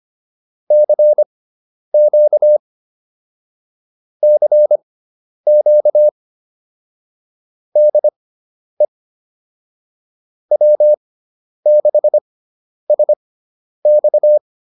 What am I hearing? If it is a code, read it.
Morse code, heard as a keyed tone, about 25 wpm, spaced as for 10 wpm, 600 Hz. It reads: CQ CQ DE W6SX